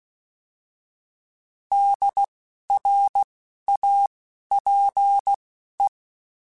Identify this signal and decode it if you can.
Morse code, heard as a keyed tone, about 16 wpm, spaced as for 12 wpm, 800 Hz. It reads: DRAPE
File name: Alphabetical / Alphabetical mp3s / Ds @10